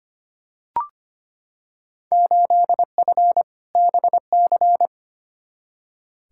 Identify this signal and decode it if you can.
Morse code, heard as a keyed tone, about 25 wpm, 700 Hz. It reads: 8F BC
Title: Field Day Exchanges Encore - 25wpm